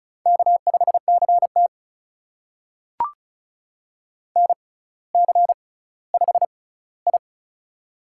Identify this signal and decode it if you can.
Morse code, heard as a keyed tone, about 36 wpm, spaced as for 18 wpm, 700 Hz. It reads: K5CT NC5I
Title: USA Call Signs - 4 Characters - 6x Character Spacing - 35wpm